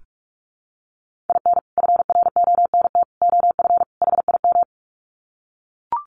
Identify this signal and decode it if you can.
Morse code, heard as a keyed tone, about 44 wpm, 700 Hz. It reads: IN FRONT OF HIM
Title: Sets of 4 Words 45wpm